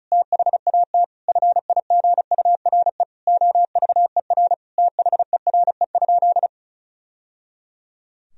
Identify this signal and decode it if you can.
Morse code, heard as a keyed tone, about 36 wpm, 700 Hz. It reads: THAT FIGURE OVER THERE?
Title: Sentences from Top 1000 Words - Review Part 2 - 35wpm